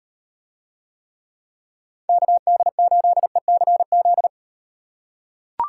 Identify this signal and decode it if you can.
Morse code, heard as a keyed tone, about 38 wpm, 700 Hz. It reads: KD8ECZ